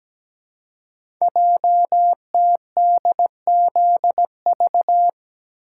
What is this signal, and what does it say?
Morse code, heard as a keyed tone, about 17 wpm, 700 Hz. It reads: JTDZV
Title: Code Groups - 5 Letters - 17wpm